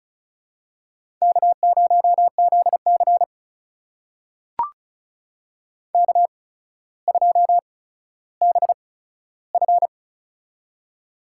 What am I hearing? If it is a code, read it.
Morse code, heard as a keyed tone, about 36 wpm, spaced as for 15 wpm, 700 Hz. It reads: K0ZC K2BF
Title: USA Call Signs - 4 Characters - 8x Character Spacing - 35wpm